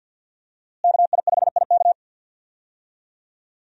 Morse code, heard as a keyed tone, about 45 wpm, 700 Hz. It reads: KI5IK
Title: USA Call Signs - 5 Characters - Repeated 2x - 50wpm